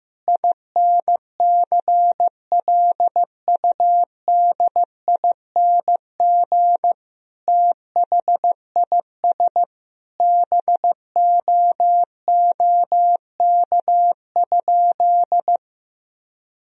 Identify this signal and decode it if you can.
Morse code, heard as a keyed tone, about 15 wpm, 700 Hz. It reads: INCLUDING THIS BOOK?
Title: Sentences from Top 1000 Words - Review Part 4 - 15wpm